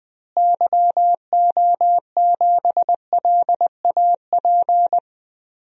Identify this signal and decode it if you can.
Morse code, heard as a keyed tone, about 20 wpm, 700 Hz. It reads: YO7LAP
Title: DXCC-100 Call Signs - 6 Characters - Repeated 3x - 20wpm